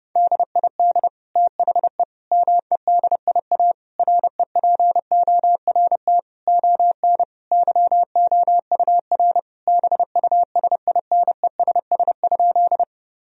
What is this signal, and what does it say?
Morse code, heard as a keyed tone, about 32 wpm, 700 Hz. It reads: DID THE MEDIA REPORT ON YOUR BUSINESS?
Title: Sentences from Top 1000 Words - Review Part 4 - 30wpm